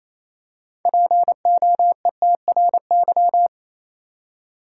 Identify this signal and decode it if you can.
Morse code, heard as a keyed tone, about 28 wpm, 700 Hz. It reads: POETRY
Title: Six Letter Words - 28wpm